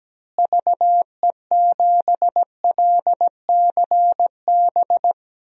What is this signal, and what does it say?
Morse code, heard as a keyed tone, about 17 wpm, 700 Hz. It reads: VE7LCB
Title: DXCC-100 Call Signs - 6 Characters - Repeated 3x - 17wpm